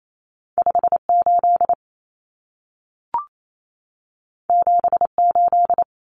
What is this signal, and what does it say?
Morse code, heard as a keyed tone, about 28 wpm, 700 Hz. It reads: 58 78
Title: Common Cut Numbers - 2-Digit 28wpm